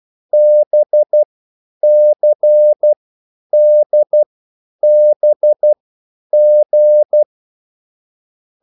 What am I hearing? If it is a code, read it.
Morse code, heard as a keyed tone, about 12 wpm, spaced as for 9 wpm, 600 Hz. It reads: BCDBG
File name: QSL / CW lesson